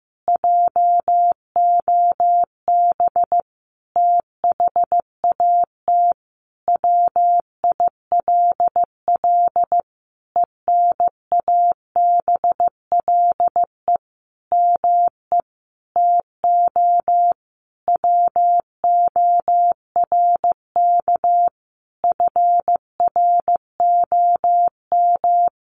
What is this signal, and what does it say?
Morse code, heard as a keyed tone, about 15 wpm, 700 Hz. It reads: JOB THAT WILL ENABLE ME TO WORK FROM
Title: Sentences from Top 1500 Words 15wpm